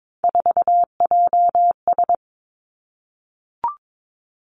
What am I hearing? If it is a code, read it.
Morse code, heard as a keyed tone, about 22 wpm, 700 Hz. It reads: 4JS
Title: Three Challenging Letter-Number Combinations 22wpm